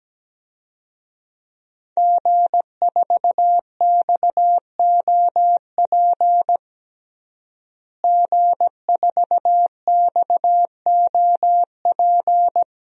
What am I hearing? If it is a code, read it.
Morse code, heard as a keyed tone, about 17 wpm, 700 Hz. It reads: G4XOP G4XOP
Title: DXCC-100 Call Signs - 5 Characters - Repeated 2x - 17wpm